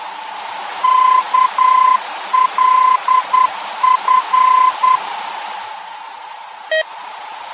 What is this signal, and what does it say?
Morse code, heard as a keyed tone, about 10 wpm, 1020 Hz. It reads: KLF
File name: germany / mx / KLF-D-VOR